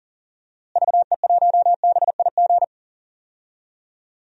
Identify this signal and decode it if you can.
Morse code, heard as a keyed tone, about 37 wpm, 700 Hz. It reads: VE1BIG
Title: DXCC-100 Call Signs - 6 Characters - Repeated 3x - 40wpm